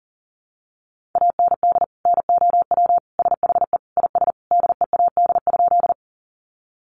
Morse code, heard as a keyed tone, about 40 wpm, 700 Hz. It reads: AND NOW SHE IS DEAD?
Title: Sentences from Top 1500 Words 40wpm